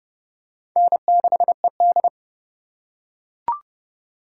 Morse code, heard as a keyed tone, about 30 wpm, 700 Hz. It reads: N6ED
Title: USA Call Signs - 4 Characters - 2x Character Spacing - 30wpm